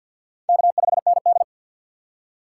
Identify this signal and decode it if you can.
Morse code, heard as a keyed tone, about 60 wpm, 700 Hz. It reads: K5ND
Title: Speed Racing - USA Call Signs 50wpm